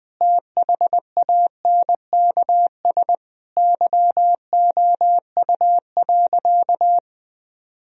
Morse code, heard as a keyed tone, about 21 wpm, 700 Hz. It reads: THANKS YOU.
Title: Sentences from Top 1000 Words - Review Part 3 - 20wpm